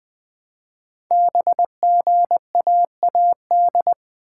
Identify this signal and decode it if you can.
Morse code, heard as a keyed tone, about 20 wpm, 700 Hz. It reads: BGAAD
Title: Code Groups - 5 Letters - 20wpm